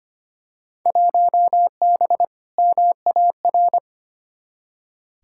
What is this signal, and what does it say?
Morse code, heard as a keyed tone, about 25 wpm, 700 Hz. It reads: 1B MAR
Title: Field Day Exchanges Encore - 25wpm